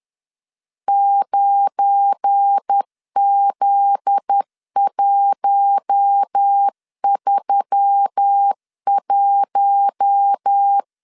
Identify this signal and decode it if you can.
Morse code, heard as a keyed tone, about 10 wpm, 800 Hz. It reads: 9Z131
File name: CW / DET / 1 9 1500C S A 10wpm 4511